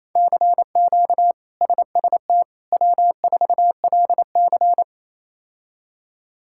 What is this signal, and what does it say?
Morse code, heard as a keyed tone, about 28 wpm, 700 Hz. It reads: CQ SST W4LC
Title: K1USN SST QSO Practice - 28wpm